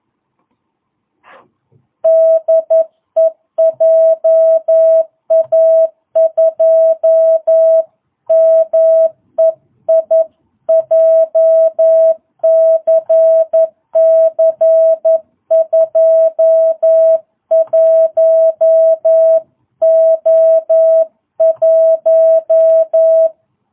Morse code, heard as a keyed tone, about 11 wpm, 660 Hz. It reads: DE JA2MEI JCC 21O1